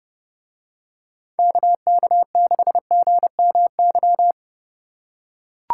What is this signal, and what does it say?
Morse code, heard as a keyed tone, about 30 wpm, 700 Hz. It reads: KK6GMY